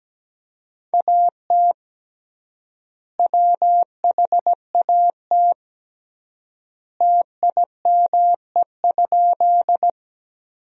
Morse code, heard as a keyed tone, about 17 wpm, 700 Hz. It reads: AT WHAT TIME?
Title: 3x Word-Spacing - Sentences from Top 100 Words - 17wpm